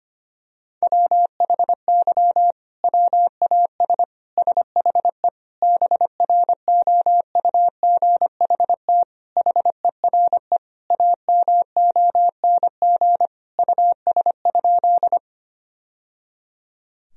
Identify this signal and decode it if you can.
Morse code, heard as a keyed tone, about 25 wpm, 700 Hz. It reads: WHY WAS SHE BROUGHT HERE AMONG US?
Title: Sentences from Top 1000 Words - Review Part 3 - 25wpm